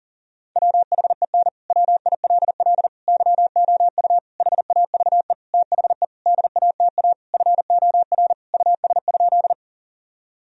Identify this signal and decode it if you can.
Morse code, heard as a keyed tone, about 40 wpm, 700 Hz. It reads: WHEN WILL YOU HAVE THE DATA FOR US?
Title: Sentences from Top 1000 Words - Review Part 3 - 40wpm